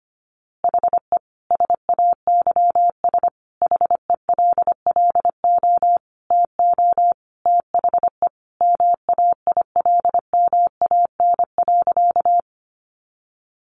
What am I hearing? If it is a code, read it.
Morse code, heard as a keyed tone, about 25 wpm, 700 Hz. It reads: HE SAYS HELLO TO THE MAILMAN.
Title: Easy Phrases 25wpm